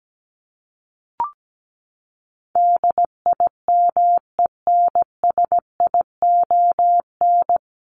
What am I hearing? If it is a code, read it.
Morse code, heard as a keyed tone, about 17 wpm, 700 Hz. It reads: DIMENSION
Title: Top 1501-1750 Words 17wpm